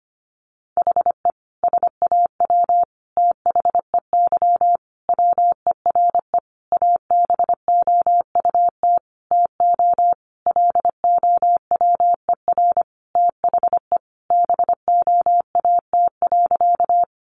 Morse code, heard as a keyed tone, about 25 wpm, 700 Hz. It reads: HE SAW THEY WERE ABOUT TO LOWER THE BOAT.
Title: Sentences from Top 700 Words 25wpm